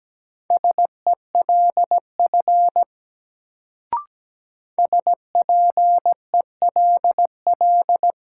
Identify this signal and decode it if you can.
Morse code, heard as a keyed tone, about 17 wpm, 700 Hz. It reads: SELF SPELL
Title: Sentences from Top 1000 Words - Review Part 1 - New Word Drill - 17wpm